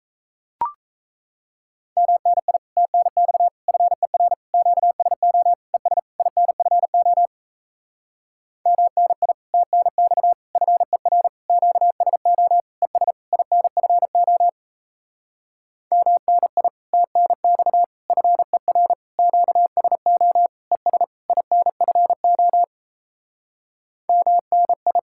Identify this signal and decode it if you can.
Morse code, heard as a keyed tone, about 37 wpm, 700 Hz. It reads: MNI TNX FER QSO ES INFO MNI TNX FER QSO ES INFO MNI TNX FER QSO ES INFO MNI